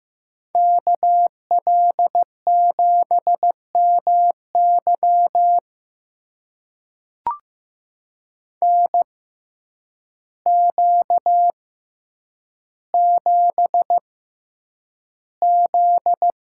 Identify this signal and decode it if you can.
Morse code, heard as a keyed tone, about 15 wpm, spaced as for 8 wpm, 700 Hz. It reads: KL7MY NQ7Z
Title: USA Call Signs - 6x Character Spacing - 15wpm